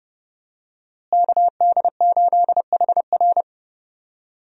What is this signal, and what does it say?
Morse code, heard as a keyed tone, about 31 wpm, 700 Hz. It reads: KD8HR
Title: USA Call Signs - 5 Characters - Repeated 3x - 30wpm